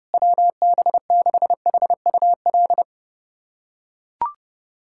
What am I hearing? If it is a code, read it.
Morse code, heard as a keyed tone, about 30 wpm, 700 Hz. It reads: WB6HUL
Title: USA Callsign DE Callsign 30wpm